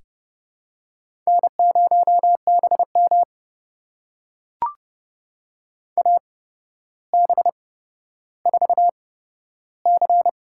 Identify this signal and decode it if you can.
Morse code, heard as a keyed tone, about 31 wpm, spaced as for 13 wpm, 700 Hz. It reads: N0BM AB4C
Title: USA Call Signs - 4 Characters - 8x Character Spacing - 30wpm